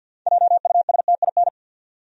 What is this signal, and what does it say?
Morse code, heard as a keyed tone, about 46 wpm, 700 Hz. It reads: JUSTIN
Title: USA Common Names - 50wpm